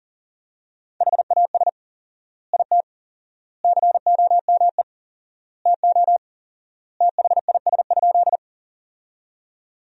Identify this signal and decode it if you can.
Morse code, heard as a keyed tone, about 40 wpm, 700 Hz. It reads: HAS IT COME TO THIS?
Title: 4x Word-Spacing - Sentences from Top 100 Words - 40wpm